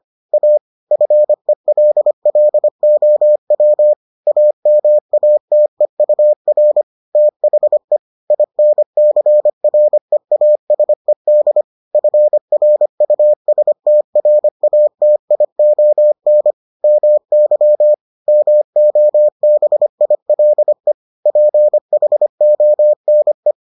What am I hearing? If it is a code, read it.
Morse code, heard as a keyed tone, about 25 wpm, 600 Hz. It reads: A FELLOW AMATEUR THE INCREASED FRUSTRATION MY MOBILE PHONE